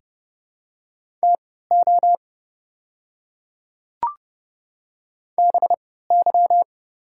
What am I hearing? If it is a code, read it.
Morse code, heard as a keyed tone, about 30 wpm, spaced as for 17 wpm, 700 Hz. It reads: TO BY
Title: 3x Character Spacing - Top 100 Words - 30wpm